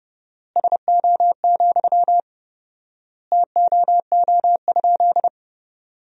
Morse code, heard as a keyed tone, about 32 wpm, 700 Hz. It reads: SO, TOO?
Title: 4x Word-Spacing - Sentences from Top 200 Words - 30wpm